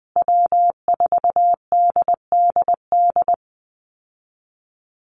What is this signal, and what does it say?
Morse code, heard as a keyed tone, about 20 wpm, 700 Hz. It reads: W4DDD
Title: Call Signs Repeated 2x - 20wpm